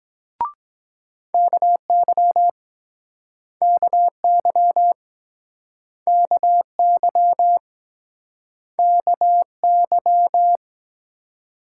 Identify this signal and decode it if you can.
Morse code, heard as a keyed tone, about 24 wpm, 700 Hz. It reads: KY KY KY KY